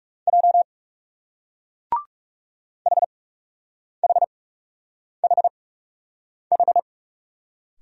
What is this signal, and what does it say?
Morse code, heard as a keyed tone, about 38 wpm, spaced as for 8 wpm, 700 Hz. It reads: J HHHH